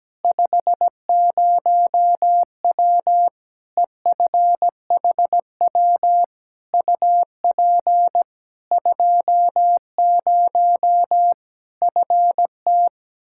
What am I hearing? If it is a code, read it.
Morse code, heard as a keyed tone, about 17 wpm, 700 Hz. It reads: 50W EFHW UP 20 FT
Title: Practice QSO - Exchange 2 - Info - 17wpm